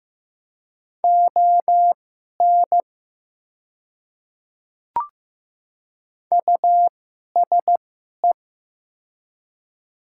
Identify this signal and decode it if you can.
Morse code, heard as a keyed tone, about 15 wpm, spaced as for 11 wpm, 700 Hz. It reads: ON USE
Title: 2x Character Spacing - Top 300 Words - 15wpm